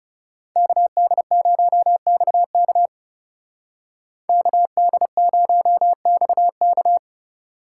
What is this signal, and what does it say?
Morse code, heard as a keyed tone, about 34 wpm, 700 Hz. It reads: KD0XK KD0XK
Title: Speed Racing - USA 2x2 Call Signs 30wpm